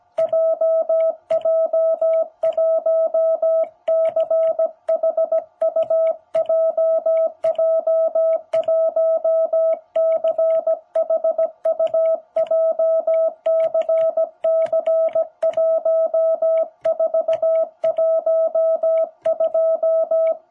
Morse code, heard as a keyed tone, about 17 wpm, 660 Hz. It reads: JJ1CHU JJ1CHU JCC1412
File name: jh8cbh / 2021cw